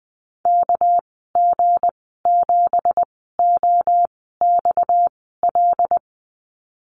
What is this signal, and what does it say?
Morse code, heard as a keyed tone, about 20 wpm, spaced as for 14 wpm, 700 Hz. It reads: KG7OXL
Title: 231 Lesson 37g - Letter x - Call Signs 20wpm